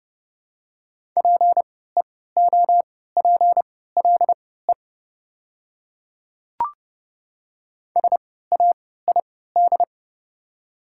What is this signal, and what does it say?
Morse code, heard as a keyed tone, about 30 wpm, spaced as for 17 wpm, 700 Hz. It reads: PEOPLE SAID
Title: 3x Character Spacing - Top 100 Words - 30wpm